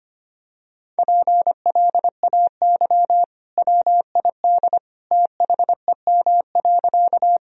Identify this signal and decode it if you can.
Morse code, heard as a keyed tone, about 25 wpm, 700 Hz. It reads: PLAY WID THEM.
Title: Sentences from Top 200 Words and Abbreviations 25wpm